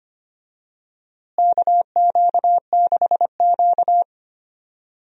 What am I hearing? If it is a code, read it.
Morse code, heard as a keyed tone, about 25 wpm, 700 Hz. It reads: KQ6Q
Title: USA Call Signs - 4 Characters - Repeated 3x - 25wpm